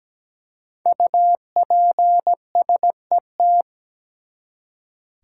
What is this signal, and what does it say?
Morse code, heard as a keyed tone, about 17 wpm, 700 Hz. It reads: UPSET
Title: Five Letter Words - 17wpm